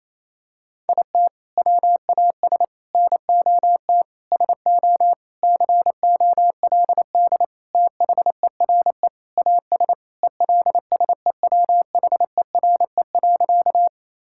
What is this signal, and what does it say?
Morse code, heard as a keyed tone, about 28 wpm, 700 Hz. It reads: IT WAS NOT SO COLD THERE AS ELSEWHERE.
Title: Sentences from Top 1600 Words 28wpm